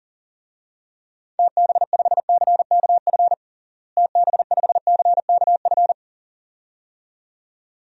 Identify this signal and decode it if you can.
Morse code, heard as a keyed tone, about 37 wpm, 700 Hz. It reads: TB5CKF TB5CKF
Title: DXCC-100 Call Signs - 6 Characters - Repeated 2x - 40wpm